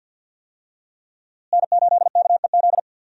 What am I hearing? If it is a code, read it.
Morse code, heard as a keyed tone, about 46 wpm, 700 Hz. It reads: N8KEZ
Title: USA Call Signs - 5 Characters - Repeated 2x - 50wpm